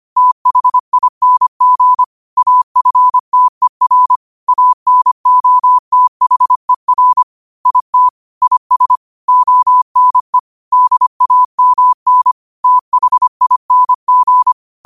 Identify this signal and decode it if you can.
Morse code, heard as a keyed tone, about 25 wpm, 1000 Hz. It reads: THING AFTER ANOTHER IT IS ONE DAMN THING
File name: qotdfiles / 25wpm